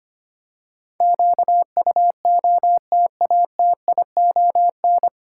Morse code, heard as a keyed tone, about 25 wpm, 700 Hz. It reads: QUOTATION
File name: practice / 25wpm / 9 Letter Words - First 3 Letters Given - 25wpm